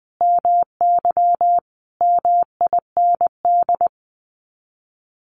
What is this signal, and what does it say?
Morse code, heard as a keyed tone, about 20 wpm, 700 Hz. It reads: MY MIND
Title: Sets of 5 Words Encore2 20wpm